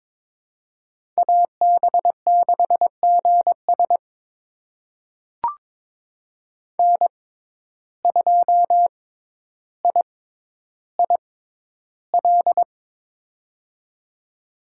Morse code, heard as a keyed tone, about 22 wpm, spaced as for 11 wpm, 700 Hz. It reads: AB6GS N2IIL
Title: USA Call Signs - 5 Characters - 6x Character Spacing - 22wpm